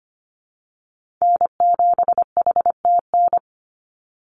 Morse code, heard as a keyed tone, about 25 wpm, 700 Hz. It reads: N7HTN
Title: USA Call Signs 25wpm